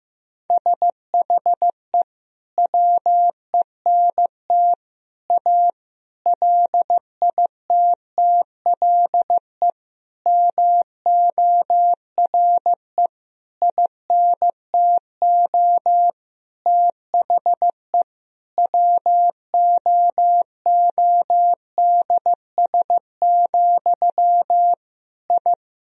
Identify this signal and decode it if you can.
Morse code, heard as a keyed tone, about 16 wpm, 700 Hz. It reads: SHE WENT A LITTLE MORE INTO THE WOODS, I
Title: Sentences from Top 1000 Words - Review Part 2 - 15wpm